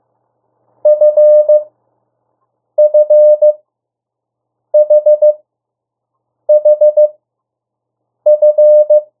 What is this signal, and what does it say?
Morse code, heard as a keyed tone, about 15 wpm, spaced as for 6 wpm, 600 Hz. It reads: FFHHF
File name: urok 1